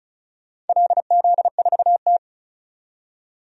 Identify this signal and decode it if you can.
Morse code, heard as a keyed tone, about 34 wpm, 700 Hz. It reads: LZ4T
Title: DXCC-100 Call Signs - 4 Characters - 35wpm